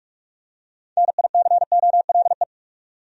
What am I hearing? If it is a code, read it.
Morse code, heard as a keyed tone, about 46 wpm, 700 Hz. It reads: NICOLE
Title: USA Common Names - 45wpm